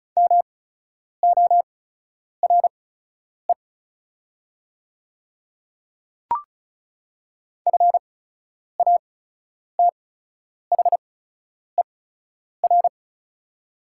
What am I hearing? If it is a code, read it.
Morse code, heard as a keyed tone, about 36 wpm, spaced as for 10 wpm, 700 Hz. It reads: MORE FATHER
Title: 8x Character Spacing - Top 300 Words - 35wpm